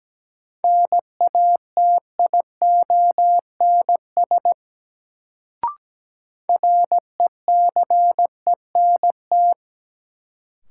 Morse code, heard as a keyed tone, about 17 wpm, 700 Hz. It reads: NATIONS RECENT
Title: Sets of 3 Words - New Word Drill - 17wpm